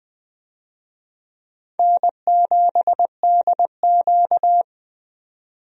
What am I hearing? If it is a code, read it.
Morse code, heard as a keyed tone, about 20 wpm, 700 Hz. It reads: N7DQ
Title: USA Call Signs - 4 Characters - 20wpm